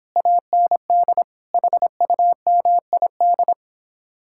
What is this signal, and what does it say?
Morse code, heard as a keyed tone, about 29 wpm, 700 Hz. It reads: AND HUMID